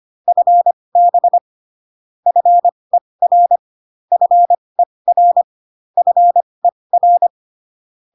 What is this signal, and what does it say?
Morse code, heard as a keyed tone, about 25 wpm, spaced as for 20 wpm, 700 Hz. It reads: FB FER FER FER